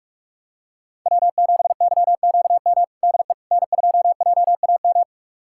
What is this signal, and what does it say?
Morse code, heard as a keyed tone, about 42 wpm, 700 Hz. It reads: W7YQM DE N2JAM
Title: USA Callsign DE Callsign - 5 Characters - 45wpm